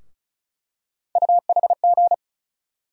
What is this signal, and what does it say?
Morse code, heard as a keyed tone, about 36 wpm, 700 Hz. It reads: UHG
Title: USA Call Sign Postfixes - 3 Letters 35wpm